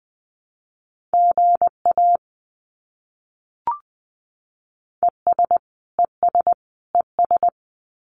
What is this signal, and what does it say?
Morse code, heard as a keyed tone, about 20 wpm, 700 Hz. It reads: GA ES ES ES
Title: Abbreviations Repeated 3x - 20wpm